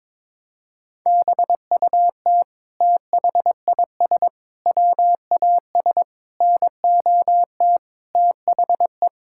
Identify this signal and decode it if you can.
Morse code, heard as a keyed tone, about 22 wpm, 700 Hz. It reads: BUT THIS WAS NOT THE
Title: Sentences from Top 1000 Words - Review Part 6 - 22wpm